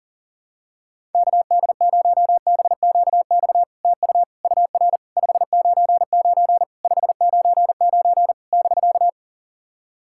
Morse code, heard as a keyed tone, about 40 wpm, 700 Hz. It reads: KD0BQX TU UR 599 599 <BK>
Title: SOTA QSO Practice 40wpm